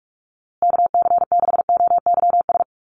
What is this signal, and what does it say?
Morse code, heard as a keyed tone, about 44 wpm, 700 Hz. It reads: KC6OYS
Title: USA Call Signs Repeated 2x 45wpm